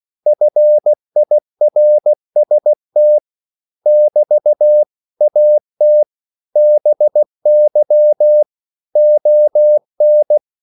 Morse code, heard as a keyed tone, about 16 wpm, 600 Hz. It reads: FIRST <BT> AT BY ON
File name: CW / Practice / 500MostCommonEnglishWords16